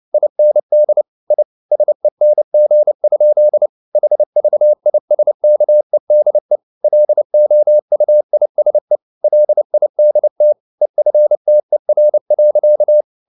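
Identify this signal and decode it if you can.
Morse code, heard as a keyed tone, about 29 wpm, 600 Hz. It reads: IND I SENG? HVISKEDE LOUISE LIDT EFTER.
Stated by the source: Fast (QRQ) Morse at 29 wpm, from CW med Gnister.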